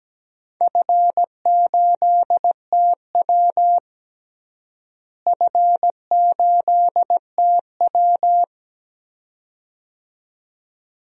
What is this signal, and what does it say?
Morse code, heard as a keyed tone, about 17 wpm, 700 Hz. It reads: F8TW F8TW
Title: DXCC-100 Call Signs - 4 Characters - Repeated 2x - 17wpm